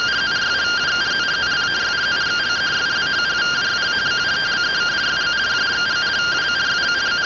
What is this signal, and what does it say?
Two-tone teleprinter signal, about 170 Hz shift.